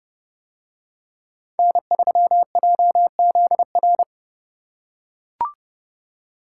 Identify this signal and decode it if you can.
Morse code, heard as a keyed tone, about 29 wpm, 700 Hz. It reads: N3JZR